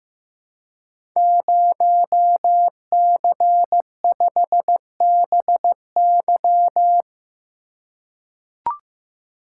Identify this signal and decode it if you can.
Morse code, heard as a keyed tone, about 15 wpm, 700 Hz. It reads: 0C5BY
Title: Code Groups - 5 Letter:Numbers - 15wpm